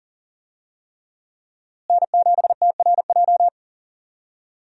Morse code, heard as a keyed tone, about 40 wpm, 700 Hz. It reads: N7TRJ
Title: USA Call Signs - 5 Characters - Repeated 3x - 40wpm